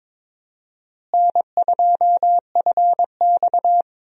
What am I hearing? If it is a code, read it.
Morse code, heard as a keyed tone, about 22 wpm, 700 Hz. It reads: N2FX
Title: USA Call Signs - 4 Characters - Repeated 2x - 22wpm